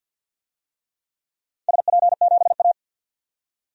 Morse code, heard as a keyed tone, about 48 wpm, 700 Hz. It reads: SP7A
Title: DXCC-100 Call Signs - 4 Characters - 50wpm